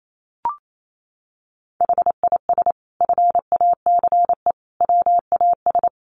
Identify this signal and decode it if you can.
Morse code, heard as a keyed tone, about 28 wpm, 700 Hz. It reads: HIS FACE WAS
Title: Sentences from Top 1200 Words 28wpm